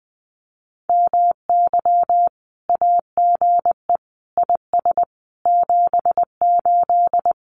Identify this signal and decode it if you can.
Morse code, heard as a keyed tone, about 20 wpm, 700 Hz. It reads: MY AGE IS 78
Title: Common QSO Phrases Encore 20wpm